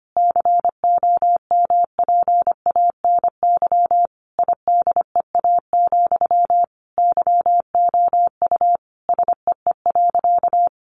Morse code, heard as a keyed tone, about 25 wpm, 700 Hz. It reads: COMPANY IDEA, YOU SEE.
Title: Sentences from Top 600 Words 25wpm